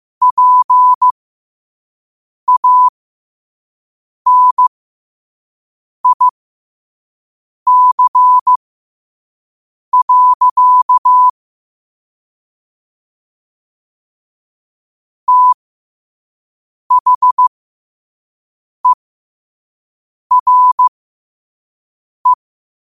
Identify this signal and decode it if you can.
Morse code, heard as a keyed tone, about 15 wpm, spaced as for 5 wpm, 1000 Hz. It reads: PANIC. THERE
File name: qotdfiles / 5wpm